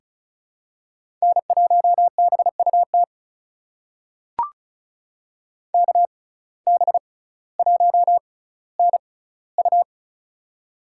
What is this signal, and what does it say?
Morse code, heard as a keyed tone, about 36 wpm, spaced as for 18 wpm, 700 Hz. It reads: N1BUT KB1NU
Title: USA Call Signs - 5 Characters - 6x Character Spacing - 35wpm